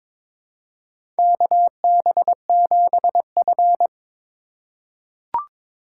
Morse code, heard as a keyed tone, about 22 wpm, 700 Hz. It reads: KB7F
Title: USA Call Signs - 2x Character Spacing - 22wpm